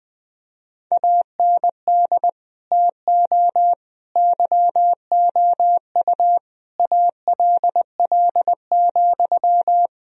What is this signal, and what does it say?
Morse code, heard as a keyed tone, about 20 wpm, 700 Hz. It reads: AND TO YOU ALL,
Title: Sentences from Top 1000 Words - Review Part 3 - 20wpm